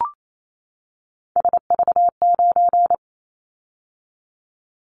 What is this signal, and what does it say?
Morse code, heard as a keyed tone, about 28 wpm, 700 Hz. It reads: SV9
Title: Callsign Prefixes for Top 100 DXCC - 28wpm